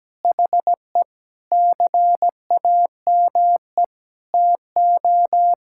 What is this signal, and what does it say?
Morse code, heard as a keyed tone, about 17 wpm, 700 Hz. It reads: HE CAME TO
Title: Sentences from Top 1000 Words - Review Part 6 - 17wpm